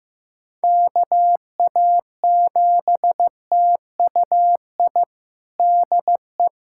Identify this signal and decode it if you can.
Morse code, heard as a keyed tone, about 15 wpm, 700 Hz. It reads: KA7TUI DE
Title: USA Callsign DE Callsign 15wpm